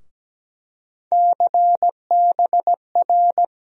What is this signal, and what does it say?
Morse code, heard as a keyed tone, about 17 wpm, 700 Hz. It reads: CBR
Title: USA Call Sign Postfixes - 3 Letters 17wpm